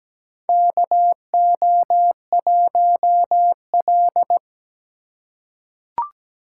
Morse code, heard as a keyed tone, about 17 wpm, 700 Hz. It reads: KO1L
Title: USA Call Signs - 4 Characters - 4x Character Spacing - 17wpm